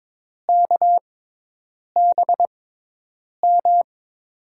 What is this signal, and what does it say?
Morse code, heard as a keyed tone, about 22 wpm, spaced as for 8 wpm, 700 Hz. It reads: KBM
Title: USA Call Signs - 6 Characters - 6x Character Spacing - 22wpm